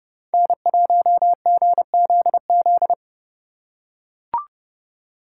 Morse code, heard as a keyed tone, about 32 wpm, 690 Hz. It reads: N1GZZ
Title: USA Call Signs - 4x Character Spacing - 30wpm